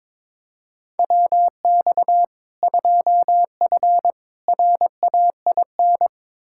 Morse code, heard as a keyed tone, about 22 wpm, 700 Hz. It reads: WX 2F RAIN
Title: K1USN SST QSO Practice 22wpm